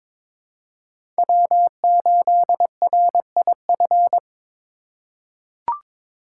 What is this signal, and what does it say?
Morse code, heard as a keyed tone, about 22 wpm, 700 Hz. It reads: W8RIF
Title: USA Call Signs - 5 Characters - 2x Character Spacing - 22wpm